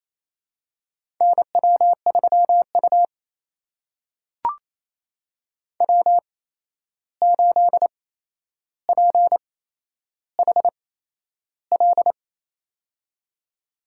Morse code, heard as a keyed tone, about 28 wpm, spaced as for 11 wpm, 700 Hz. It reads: NW3U W8PHL
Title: USA Call Signs - 8x Character Spacing - 28wpm